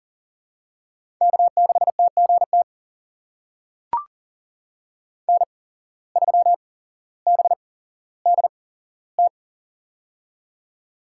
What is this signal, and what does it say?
Morse code, heard as a keyed tone, about 40 wpm, spaced as for 17 wpm, 700 Hz. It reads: K6TGT N3BDT
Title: USA Call Signs - 5 Characters - 8x Character Spacing - 40wpm